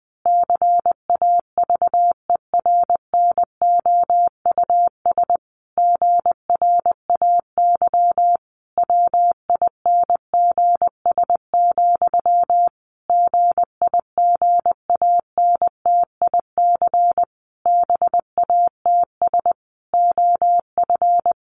Morse code, heard as a keyed tone, about 20 wpm, 700 Hz. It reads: CAVERNOUS GRAY WINGS, GIGANTIC BATS OF
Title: A Sound Of Thunder 20wpm